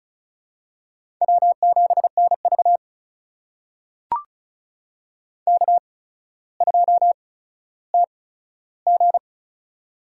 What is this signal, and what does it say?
Morse code, heard as a keyed tone, about 36 wpm, spaced as for 15 wpm, 700 Hz. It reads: W7NV K2TG
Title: USA Call Signs - 4 Characters - 8x Character Spacing - 35wpm